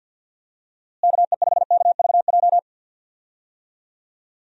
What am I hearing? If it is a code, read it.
Morse code, heard as a keyed tone, about 51 wpm, 700 Hz. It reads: KE5KVJ
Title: USA Call Signs - 6 Characters - Repeated 3x - 50wpm